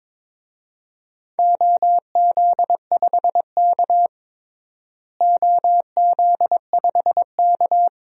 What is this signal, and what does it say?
Morse code, heard as a keyed tone, about 22 wpm, 700 Hz. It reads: OZ5K OZ5K
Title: DXCC-100 Call Signs - 4 Characters - Repeated 3x - 22wpm